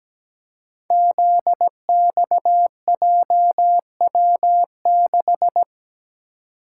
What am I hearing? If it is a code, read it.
Morse code, heard as a keyed tone, about 17 wpm, 700 Hz. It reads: ZXJW6
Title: Code Groups - 5 Letter:Numbers - 17wpm